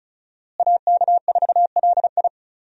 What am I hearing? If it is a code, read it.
Morse code, heard as a keyed tone, about 36 wpm, 700 Hz. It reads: AK4LI
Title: USA Call Signs - 5 Characters - Repeated 3x - 35wpm